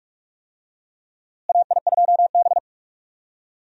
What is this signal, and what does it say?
Morse code, heard as a keyed tone, about 44 wpm, 700 Hz. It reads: AI2B
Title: USA Call Signs - 4 Characters - Repeated 2x - 45wpm